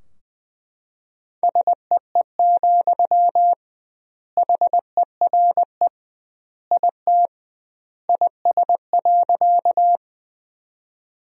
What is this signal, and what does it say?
Morse code, heard as a keyed tone, about 20 wpm, 700 Hz. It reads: SEE, HERE IT IS.
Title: 2x Word-Spacing - Sentences from Top 200 Words - 20wpm